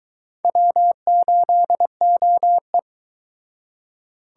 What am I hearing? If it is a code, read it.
Morse code, heard as a keyed tone, about 23 wpm, 700 Hz. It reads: W8OE